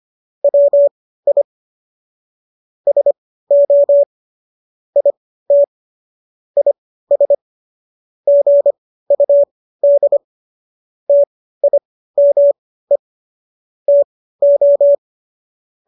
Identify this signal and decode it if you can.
Morse code, heard as a keyed tone, about 25 wpm, spaced as for 15 wpm, 570 Hz. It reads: WI SO IT IS GUD TIME TO